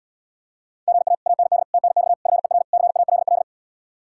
Morse code, heard as a keyed tone, about 79 wpm, 700 Hz. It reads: GA AND TNX FER CALL